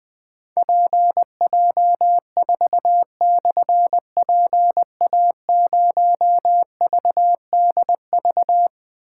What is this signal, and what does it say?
Morse code, heard as a keyed tone, about 20 wpm, 700 Hz. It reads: PJ4/PA0VDV
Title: Difficult Call Signs 20wpm